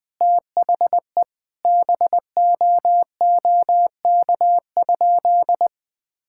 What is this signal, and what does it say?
Morse code, heard as a keyed tone, about 20 wpm, 700 Hz. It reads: THE BOOK?
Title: Sentences from Top 1000 Words - Review Part 5 - 20wpm